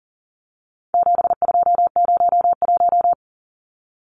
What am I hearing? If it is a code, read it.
Morse code, heard as a keyed tone, about 40 wpm, 700 Hz. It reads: 7201
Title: Numbers - 4-Digit 40wpm